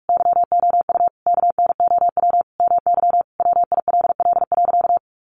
Morse code, heard as a keyed tone, about 44 wpm, 700 Hz. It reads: YOU KNOW MY WILL.
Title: Sentences from Top 1500 Words 45wpm